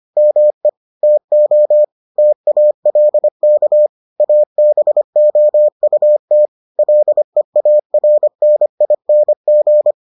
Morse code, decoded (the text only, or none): ME TO TALK ABOUT LEARNING